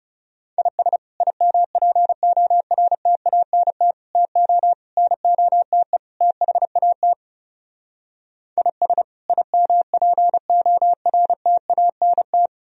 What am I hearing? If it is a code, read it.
Morse code, heard as a keyed tone, about 37 wpm, 700 Hz. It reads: IS IMPORTANT TO NOTE THAT IS IMPORTANT
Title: Speed-Racing - Sets of 5 Words 30wpm